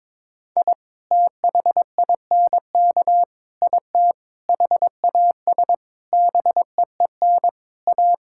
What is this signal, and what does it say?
Morse code, heard as a keyed tone, about 22 wpm, 700 Hz. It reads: I THINK IT HAS BEEN A
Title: Sentences from Top 1000 Words - Review Part 6 - 22wpm